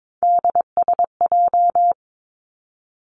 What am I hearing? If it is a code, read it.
Morse code, heard as a keyed tone, about 22 wpm, 700 Hz. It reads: DSJ
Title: Three Challenging Letter-Number Combinations 22wpm